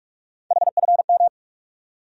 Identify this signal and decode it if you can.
Morse code, heard as a keyed tone, about 46 wpm, 700 Hz. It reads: HFM
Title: USA Call Sign Postfixes - 3 Letters 45wpm